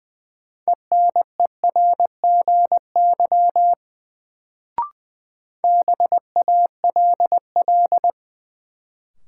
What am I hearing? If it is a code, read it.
Morse code, heard as a keyed tone, about 20 wpm, 700 Hz. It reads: ENERGY BALL
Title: Sentences from Top 900 Words - New Word Drill - 20wpm